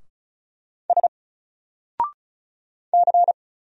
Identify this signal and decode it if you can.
Morse code, heard as a keyed tone, about 35 wpm, 700 Hz. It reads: S C
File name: Single-Letter-Number-35wpm